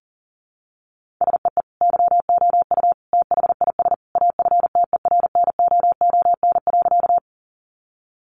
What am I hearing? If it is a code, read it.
Morse code, heard as a keyed tone, about 40 wpm, 700 Hz. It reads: SEE YOU THIS AFTERNOON.
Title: Sentences from Top 1500 Words 40wpm